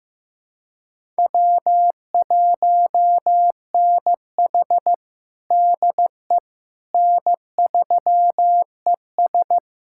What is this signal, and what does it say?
Morse code, heard as a keyed tone, about 15 wpm, 700 Hz. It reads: W1NH DE N3ES
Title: USA Callsign DE Callsign - 4 Characters - 15wpm